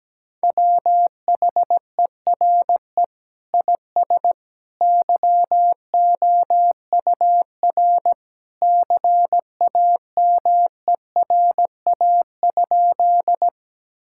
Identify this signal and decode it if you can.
Morse code, heard as a keyed tone, about 17 wpm, 700 Hz. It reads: WHERE IS YOUR CAMERA?
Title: Sentences from Top 1000 Words - Review Part 5 - 17wpm